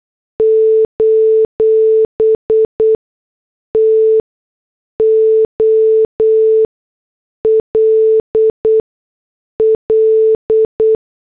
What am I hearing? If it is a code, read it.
Morse code, heard as a keyed tone, about 8 wpm, spaced as for 6 wpm, 440 Hz. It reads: : TOLL
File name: morse / headlines